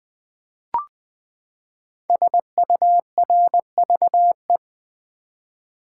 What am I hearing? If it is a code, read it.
Morse code, heard as a keyed tone, about 20 wpm, 700 Hz. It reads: SURVE
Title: Build A Word - 5 Letters At A Time - 12-Letter Words - 20wpm